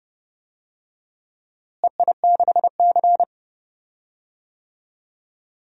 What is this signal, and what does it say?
Morse code, heard as a keyed tone, about 30 wpm, 700 Hz. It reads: EI6C
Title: DXCC-100 Call Signs - 4 Characters - Repeated 2x - 30wpm